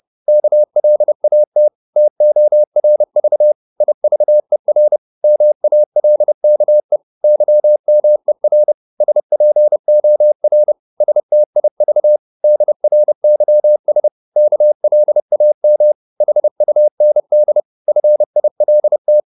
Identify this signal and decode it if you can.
Morse code, heard as a keyed tone, about 30 wpm, 600 Hz. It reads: KLAT TORV IVER MALKE YMER SPOR STIV DRYS KLAM HUND FILT
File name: Korte ord DK 30wpm